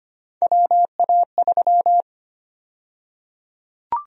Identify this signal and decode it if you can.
Morse code, heard as a keyed tone, about 25 wpm, 700 Hz. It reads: WA3
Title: USA Call Sign Prefixes - Repeated 2x - 25wpm